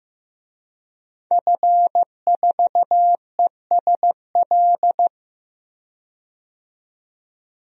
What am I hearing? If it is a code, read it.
Morse code, heard as a keyed tone, about 15 wpm, 700 Hz. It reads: F4ESL
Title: DXCC-100 Call Signs - 5 Characters - 15wpm